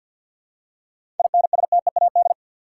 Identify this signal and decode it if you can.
Morse code, heard as a keyed tone, about 51 wpm, 700 Hz. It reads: INSTEAD
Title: Seven Letter Words - 50wpm